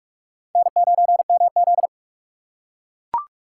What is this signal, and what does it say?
Morse code, heard as a keyed tone, about 44 wpm, 700 Hz. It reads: N9MZ
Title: USA Call Signs - 4 Characters - 2x Character Spacing - 45wpm